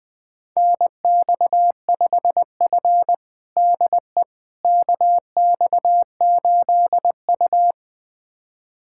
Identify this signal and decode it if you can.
Morse code, heard as a keyed tone, about 20 wpm, 700 Hz. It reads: NX5F DE KX8U
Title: USA Callsign DE Callsign - 4 Characters - 20wpm